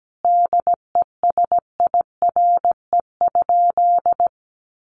Morse code, heard as a keyed tone, about 17 wpm, 700 Hz. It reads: DESIRE?
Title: Sentences from Top 1100 Words 17wpm